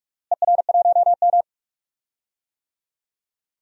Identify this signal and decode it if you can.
Morse code, heard as a keyed tone, about 46 wpm, 700 Hz. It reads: ER1M
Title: DXCC-100 Call Signs - 4 Characters - Repeated 3x - 45wpm